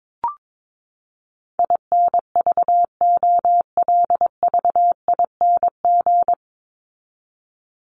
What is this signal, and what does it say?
Morse code, heard as a keyed tone, about 22 wpm, 700 Hz. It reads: INVOLVING
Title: ING Suffix - Top 100 Words with Base Words 22wpm